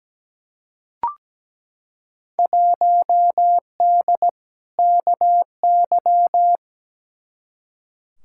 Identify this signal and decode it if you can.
Morse code, heard as a keyed tone, about 17 wpm, 700 Hz. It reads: 1D KY
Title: Field Day Exchanges - 17wpm